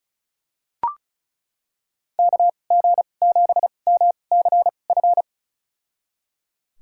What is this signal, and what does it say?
Morse code, heard as a keyed tone, about 36 wpm, spaced as for 25 wpm, 700 Hz. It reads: KG7MCF
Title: USA Call Signs - 6 Characters - 2x Character Spacing - 35wpm